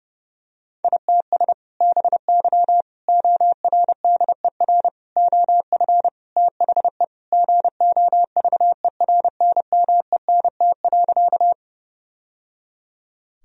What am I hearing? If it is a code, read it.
Morse code, heard as a keyed tone, about 32 wpm, 700 Hz. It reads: ITS BY ORDER OF THE GOVERNMENT.
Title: Sentences from Top 1000 Words - Review Part 3 - 30wpm